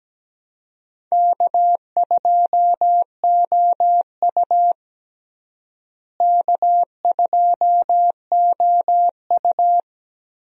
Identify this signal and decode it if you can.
Morse code, heard as a keyed tone, about 17 wpm, 700 Hz. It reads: K2OU K2OU
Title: USA Call Signs - 4 Characters - Repeated 3x - 17wpm